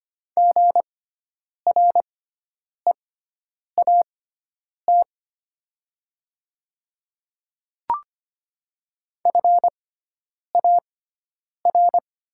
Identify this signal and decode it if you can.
Morse code, heard as a keyed tone, about 25 wpm, spaced as for 9 wpm, 700 Hz. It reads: GREAT FAR
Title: 6x Character Spacing - Top 500 Words - 25wpm